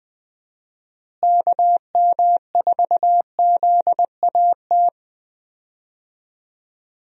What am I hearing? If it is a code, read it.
Morse code, heard as a keyed tone, about 21 wpm, 700 Hz. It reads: KM4ZAT
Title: USA Call Signs - 6 Characters - Repeated 3x - 20wpm